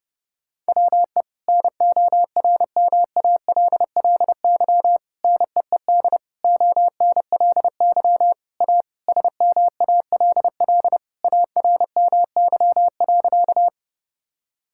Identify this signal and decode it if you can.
Morse code, heard as a keyed tone, about 30 wpm, 700 Hz. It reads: WE NORMALLY NEED ONLY A SMALL ARMY.
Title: Sentences from Top 1000 Words - Review Part 6 - 30wpm